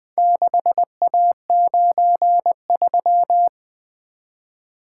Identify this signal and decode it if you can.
Morse code, heard as a keyed tone, about 20 wpm, 700 Hz. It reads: 6A93
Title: Four Letter-Number Combinations - 20wpm